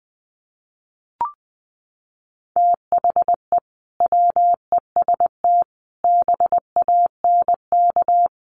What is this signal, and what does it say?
Morse code, heard as a keyed tone, about 20 wpm, 700 Hz. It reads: THE WEST BANK
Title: Sets of 5 Words 20wpm